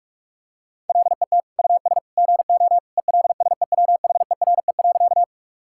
Morse code, heard as a keyed tone, about 41 wpm, 700 Hz. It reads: LET US GO ELSEWHERE.
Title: Sentences from Top 1600 Words 45wpm